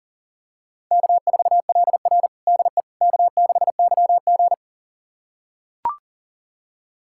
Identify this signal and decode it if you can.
Morse code, heard as a keyed tone, about 37 wpm, 700 Hz. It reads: K4LR DE K6YG
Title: USA Callsign DE Callsign - 4 Characters - 40wpm